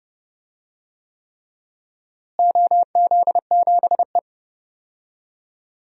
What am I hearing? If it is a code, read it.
Morse code, heard as a keyed tone, about 32 wpm, 700 Hz. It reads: OZ7E
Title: DXCC-100 Call Signs - 4 Characters - Repeated 3x - 30wpm